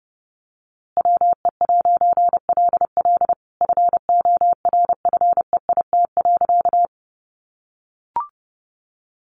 Morse code, heard as a keyed tone, about 30 wpm, 700 Hz. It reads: WE'LL FORFEIT.
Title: A Sound Of Thunder 30wpm